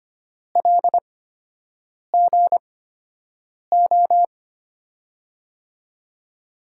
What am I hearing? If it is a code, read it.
Morse code, heard as a keyed tone, about 25 wpm, spaced as for 7 wpm, 700 Hz. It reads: LGO
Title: USA Call Signs - 6 Characters - 8x Character Spacing - 25wpm